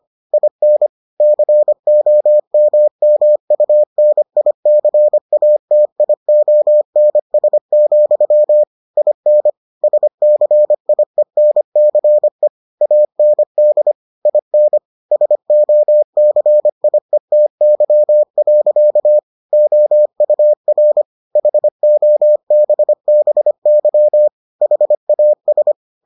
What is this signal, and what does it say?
Morse code, heard as a keyed tone, about 25 wpm, 600 Hz. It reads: IN COMMUNICATIONS, IN SCIENCE AND IN SOCIETY. OUR HOBBY HAS